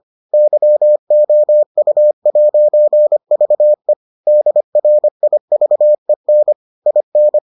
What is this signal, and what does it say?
Morse code, heard as a keyed tone, about 25 wpm, 600 Hz. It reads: YOU'VE DRIVEN IN